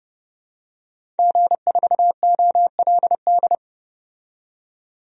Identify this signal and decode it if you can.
Morse code, heard as a keyed tone, about 31 wpm, 700 Hz. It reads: G4OLD
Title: DXCC-100 Call Signs - 5 Characters - 30wpm